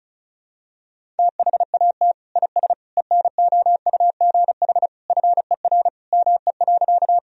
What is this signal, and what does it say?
Morse code, heard as a keyed tone, about 34 wpm, 700 Hz. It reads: THAT IS ENOUGH FER ME.
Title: Sentences from Top 400 Words and Abbreviations 35wpm